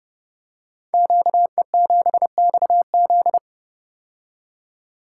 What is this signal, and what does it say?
Morse code, heard as a keyed tone, about 30 wpm, 700 Hz. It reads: QE7XZ
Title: Code Groups - 5 Letter:Numbers - 30wpm